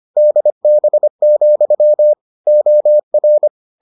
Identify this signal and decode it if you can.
Morse code, heard as a keyed tone, about 25 wpm, 600 Hz. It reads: DB, OR